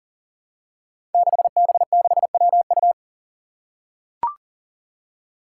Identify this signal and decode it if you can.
Morse code, heard as a keyed tone, about 43 wpm, 700 Hz. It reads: 6B6WU
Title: Five Challenging Letter-Number Combinations 40wpm